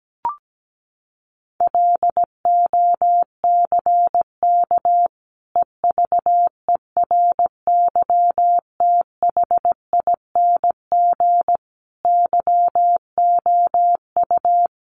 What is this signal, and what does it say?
Morse code, heard as a keyed tone, about 17 wpm, 700 Hz. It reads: LOCK EVERYTHING YOU
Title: Sentences from Top 1300 Words 17wpm